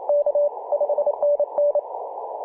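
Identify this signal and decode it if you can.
Morse code, heard as a keyed tone, about 28 wpm, 580 Hz, with noise. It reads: K 5NN
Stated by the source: P4/DL6RAI heard on CW on 7.003 MHz, recorded by ja0rug.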